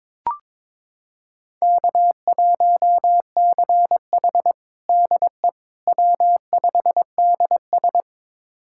Morse code, heard as a keyed tone, about 21 wpm, 700 Hz. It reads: K1CH DE W5DS
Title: USA Callsign DE Callsign - 4 Characters - 22wpm